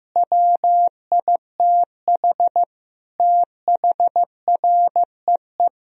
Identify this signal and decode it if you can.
Morse code, heard as a keyed tone, about 16 wpm, 700 Hz. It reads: WITH THREE
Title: Sentences from Top 1000 Words - Review Part 1 - 15wpm